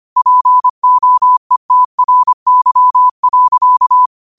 Morse code, heard as a keyed tone, about 25 wpm, 1000 Hz. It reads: POETRY.